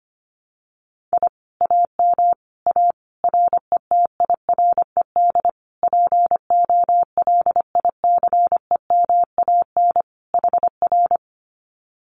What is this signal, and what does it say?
Morse code, heard as a keyed tone, about 25 wpm, 700 Hz. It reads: I AM A RETIRED POLICEMAN HR
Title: Common QSO Phrases Encore 25wpm